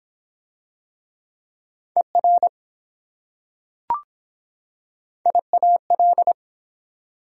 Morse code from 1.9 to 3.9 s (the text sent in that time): ER